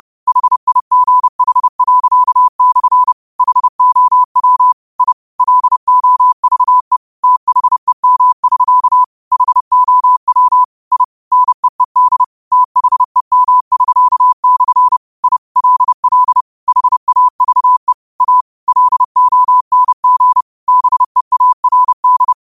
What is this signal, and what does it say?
Morse code, heard as a keyed tone, about 30 wpm, 1000 Hz. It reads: HIGH./ HOW I LOVE THEM<SK> HOW I NEED THEM<SK>/ I LL HAVE A LONG BEARD